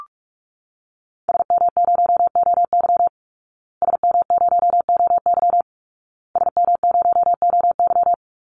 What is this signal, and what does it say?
Morse code, heard as a keyed tone, about 44 wpm, 700 Hz. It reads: SM0OY SM0OY SM0OY
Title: Call Signs Repeated 2x - 45wpm